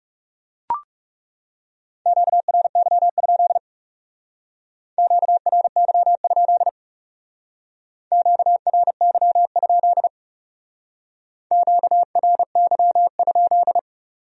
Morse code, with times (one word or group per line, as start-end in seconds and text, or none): 2.1-3.6 QRY?
5.0-6.7 QRY?
8.1-10.1 QRY?
11.5-13.8 QRY?